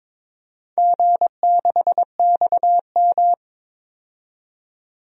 Morse code, heard as a keyed tone, about 22 wpm, 700 Hz. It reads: G6XM
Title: DXCC-100 Call Signs - 4 Characters - 22wpm